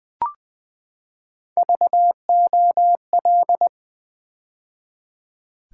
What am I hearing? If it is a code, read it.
Morse code, heard as a keyed tone, about 20 wpm, 700 Hz. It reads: VOL
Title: Build A Word - 3 Letters At A Time - 6-Letter Words - 20wpm